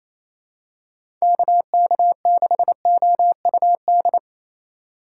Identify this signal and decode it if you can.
Morse code, heard as a keyed tone, about 28 wpm, 700 Hz. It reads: KK6OUD
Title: USA Call Signs - 6 Characters - 2x Character Spacing - 28wpm